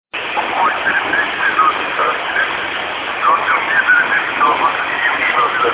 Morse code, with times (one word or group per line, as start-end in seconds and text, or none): none